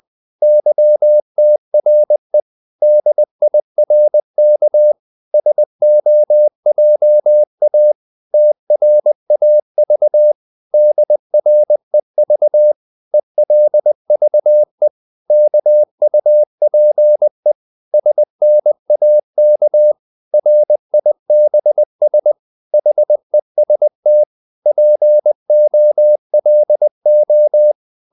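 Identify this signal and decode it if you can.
Morse code, heard as a keyed tone, about 20 wpm, 600 Hz. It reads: YTRE DIRK SOJA TRAV DREV ELVE KUPE SNAK RIBS HEST POLO
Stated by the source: Morse code practice of short Danish words at 20 wpm, from CW med Gnister.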